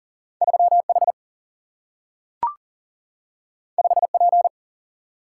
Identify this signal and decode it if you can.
Morse code, heard as a keyed tone, about 39 wpm, 700 Hz. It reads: 3H 5P
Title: Single-Letter Single-Number 40wpm